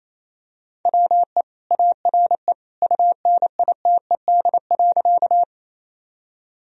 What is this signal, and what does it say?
Morse code, heard as a keyed tone, about 28 wpm, 700 Hz. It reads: WE ARE UNITED.
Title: Sentences from Top 1000 Words - Review Part 4 - 28wpm